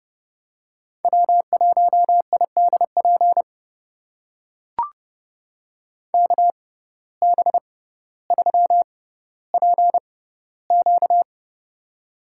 Morse code, heard as a keyed tone, about 30 wpm, spaced as for 15 wpm, 700 Hz. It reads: W1IDP KB3PQ
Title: USA Call Signs - 6x Character Spacing - 30wpm